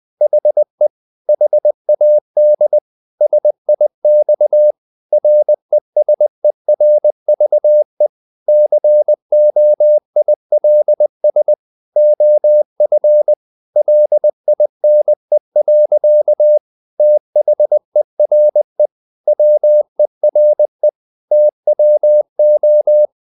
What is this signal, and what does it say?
Morse code, heard as a keyed tone, about 20 wpm, 600 Hz. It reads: HE HAD SIX RESERVE COILS OF LINE. THERE WERE TWO
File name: the old man and the sea / the old man and the sea part 1 20wpm